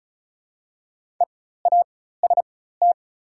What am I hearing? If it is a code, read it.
Morse code, heard as a keyed tone, about 34 wpm, spaced as for 16 wpm, 700 Hz. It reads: EAST
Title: 4x Character Spacing - Top 500 Words - 35wpm